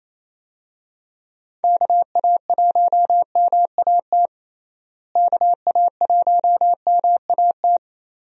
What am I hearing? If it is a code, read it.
Morse code, heard as a keyed tone, about 28 wpm, 700 Hz. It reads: KA1MAT KA1MAT
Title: USA Call Signs - 6 Characters - Repeated 3x - 28wpm